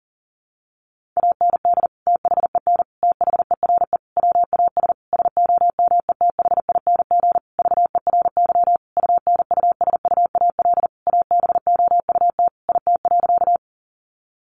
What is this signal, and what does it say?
Morse code, heard as a keyed tone, about 40 wpm, 700 Hz. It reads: AND THEN THERE WAS SOMETHING VERY UNUSUAL ABOUT IT.
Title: Sentences from Top 1400 Words 40wpm